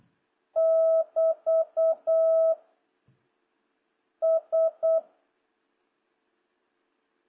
Morse code, heard as a keyed tone, about 8 wpm, 650 Hz. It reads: <BT> S
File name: jh8cbh / 2026cw